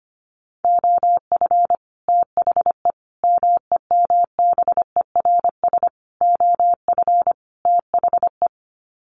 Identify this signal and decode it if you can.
Morse code, heard as a keyed tone, about 25 wpm, 700 Hz. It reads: OF THE MEMBERS OF THE
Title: Sets of 5 Words Encore2 25wpm